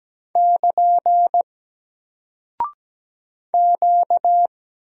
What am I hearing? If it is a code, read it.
Morse code, heard as a keyed tone, about 17 wpm, 700 Hz. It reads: <KN> Q
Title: Single Character (Letter:Number:Punctuation:Prosign) Alt 17wpm